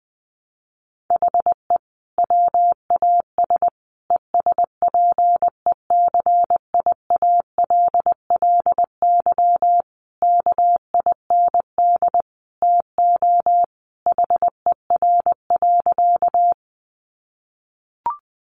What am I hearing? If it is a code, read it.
Morse code, heard as a keyed tone, about 20 wpm, 700 Hz. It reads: HE WAS ESPECIALLY KIND TO HER.
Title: Sentences from Top 700 Words 20wpm